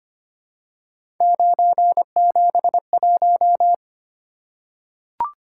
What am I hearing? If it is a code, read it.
Morse code, heard as a keyed tone, about 25 wpm, 700 Hz. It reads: 971
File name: Three Letter-Number Combinations - 25wpm